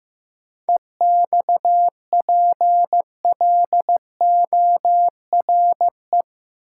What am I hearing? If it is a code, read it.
Morse code, heard as a keyed tone, about 15 wpm, 700 Hz. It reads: EXPLORE
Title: Sentences from Top 1100 Words - New Word Drill Alt - 15wpm